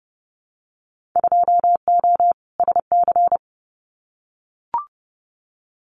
Morse code, heard as a keyed tone, about 30 wpm, 700 Hz. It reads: 2O SC
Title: Winter Field Day Exchanges Encore 30wpm